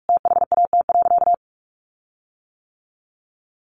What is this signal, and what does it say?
Morse code, heard as a keyed tone, about 44 wpm, 700 Hz. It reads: THAT.
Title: Sentences from Top 1400 Words 45wpm